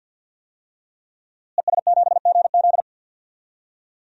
Morse code, heard as a keyed tone, about 52 wpm, 700 Hz. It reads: ES7GZ
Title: DXCC-100 Call Signs - 5 Characters - Repeated 2x - 50wpm